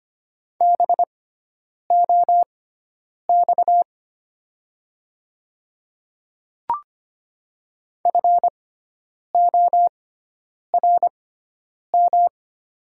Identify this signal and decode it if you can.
Morse code, heard as a keyed tone, about 25 wpm, spaced as for 9 wpm, 700 Hz. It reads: BOX FORM
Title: 6x Character Spacing - Top 500 Words - 25wpm